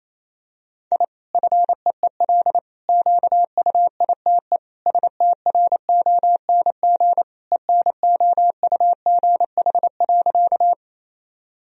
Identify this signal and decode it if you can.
Morse code, heard as a keyed tone, about 28 wpm, 700 Hz. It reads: I FEEL QUITE STRONG ENOUGH.
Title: Sentences from Top 1000 Words - Review Part 3 - 28wpm